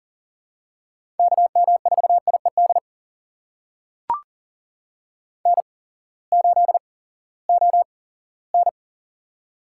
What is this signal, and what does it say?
Morse code, heard as a keyed tone, about 40 wpm, spaced as for 20 wpm, 700 Hz. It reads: KM4IED N8ON
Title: USA Call Signs - 8x Character Spacing - 40wpm